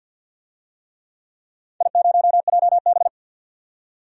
Morse code, heard as a keyed tone, about 52 wpm, 700 Hz. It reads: I0JB
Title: DXCC-100 Call Signs - 4 Characters - 50wpm